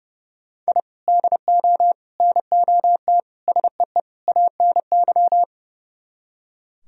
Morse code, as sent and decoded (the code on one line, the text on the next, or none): .. / -.. --- / -. --- - / ... . / .- -. -.--
I DO NOT SEE ANY